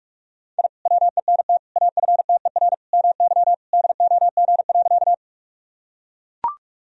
Morse code, heard as a keyed tone, about 44 wpm, 700 Hz. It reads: I WENT AFTER MY DOG.
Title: Sentences from Top 1000 Words - Review Part 2 - 45wpm